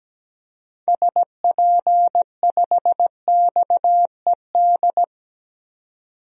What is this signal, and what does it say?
Morse code, heard as a keyed tone, about 17 wpm, 700 Hz. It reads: SP5XED
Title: DXCC-100 Call Signs - 6 Characters - 17wpm